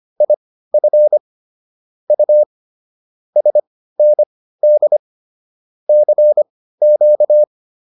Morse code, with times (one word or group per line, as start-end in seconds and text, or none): 0.2-1.2 IF
2.1-2.4 U
3.4-5.0 SND
5.9-7.4 CQ